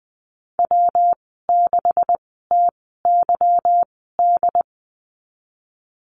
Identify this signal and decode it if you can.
Morse code, heard as a keyed tone, about 19 wpm, spaced as for 14 wpm, 700 Hz. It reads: W6TYD
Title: 147 Lesson 24g - Letter y - Call Signs 20wpm